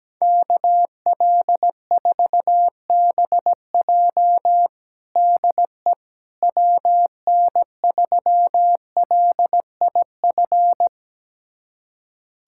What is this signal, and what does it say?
Morse code, heard as a keyed tone, about 17 wpm, 700 Hz. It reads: KL4BJ DE WN3LIF
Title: USA Callsign DE Callsign 17wpm